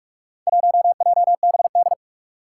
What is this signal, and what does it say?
Morse code, heard as a keyed tone, about 42 wpm, 700 Hz. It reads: 1JBD
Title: Four Challenging Letter-Number Combinations 45wpm